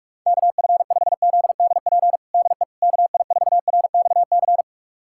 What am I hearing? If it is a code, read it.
Morse code, heard as a keyed tone, about 47 wpm, 690 Hz. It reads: KF5ZDP DE KI4RXC
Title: USA Callsign DE Callsign 45wpm